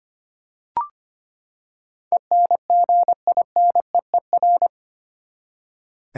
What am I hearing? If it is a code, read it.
Morse code, heard as a keyed tone, about 25 wpm, 700 Hz. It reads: ENGINEER
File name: Eight Letter Words - 25wpm